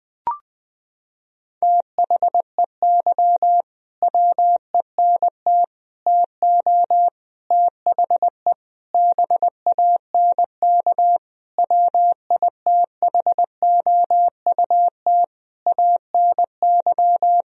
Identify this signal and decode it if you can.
Morse code, heard as a keyed tone, about 20 wpm, 700 Hz. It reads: THEY WENT TO THE BANK WITHOUT ANY
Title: Sentences from Top 1000 Words - Review Part 4 - 20wpm